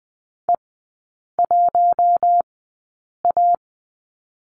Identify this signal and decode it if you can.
Morse code, heard as a keyed tone, about 20 wpm, spaced as for 8 wpm, 700 Hz. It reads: E1A
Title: 053 Lesson 10e - Letter r - Three Characters 20wpm